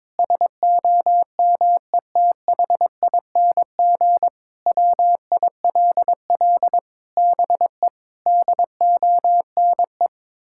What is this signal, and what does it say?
Morse code, heard as a keyed tone, about 22 wpm, 700 Hz. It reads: SOMETHING WILL BE DONE
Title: Sentences from Top 1600 Words 22wpm